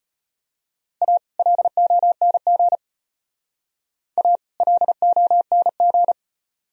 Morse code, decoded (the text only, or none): A LONG A LONG